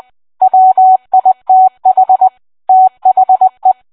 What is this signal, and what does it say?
Morse code, heard as a keyed tone, about 20 wpm, 750 Hz. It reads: WITH THE